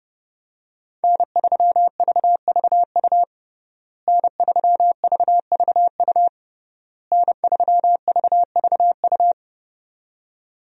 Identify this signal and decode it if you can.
Morse code, heard as a keyed tone, about 30 wpm, 700 Hz. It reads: N3VVU N3VVU N3VVU
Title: USA Call Signs - 5 Characters - Repeated 3x - 30wpm